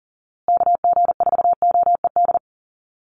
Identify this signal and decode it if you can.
Morse code, heard as a keyed tone, about 40 wpm, 700 Hz. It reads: KG4OED
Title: USA Call Signs Repeated 3x 40wpm